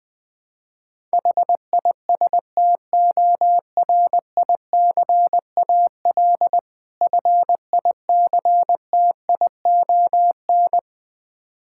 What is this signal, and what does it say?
Morse code, heard as a keyed tone, about 20 wpm, 690 Hz. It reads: HISTORICAL FICTION
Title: Difficult Words with Hints - 20wpm